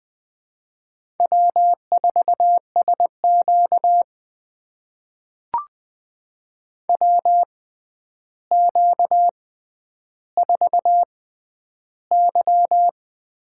Morse code, heard as a keyed tone, about 20 wpm, spaced as for 10 wpm, 700 Hz. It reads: W4SQ WQ4Y
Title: USA Call Signs - 4 Characters - 6x Character Spacing - 20wpm